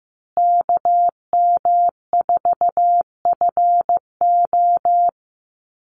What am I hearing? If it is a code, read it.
Morse code, heard as a keyed tone, about 15 wpm, 700 Hz. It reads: KM4FO
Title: Call Signs Repeated 2x - 15wpm